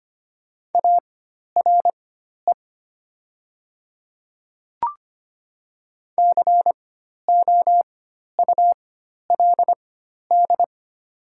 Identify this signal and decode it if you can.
Morse code, heard as a keyed tone, about 25 wpm, spaced as for 12 wpm, 700 Hz. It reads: ARE COULD
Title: 4x Character Spacing - Top 300 Words - 25wpm